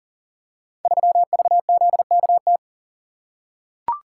Morse code, heard as a keyed tone, about 42 wpm, 700 Hz. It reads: 3VZKT
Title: Code Groups - 5 Letter:Numbers - 40wpm